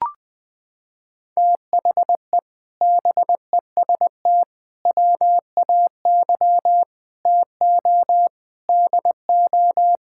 Morse code, heard as a keyed tone, about 20 wpm, 700 Hz. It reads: THE BEST WAY TO DO
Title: Sets of 8 Words - 20wpm